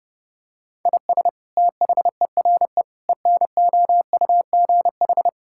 Morse code, heard as a keyed tone, about 30 wpm, 700 Hz. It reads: IS THERE ENOUGH
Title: Sentences from Top 1000 Words - Review Part 5 - 30wpm